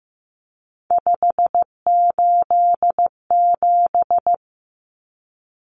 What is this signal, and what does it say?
Morse code, heard as a keyed tone, about 15 wpm, 700 Hz. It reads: 587
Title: Numbers - 3-Digit 15wpm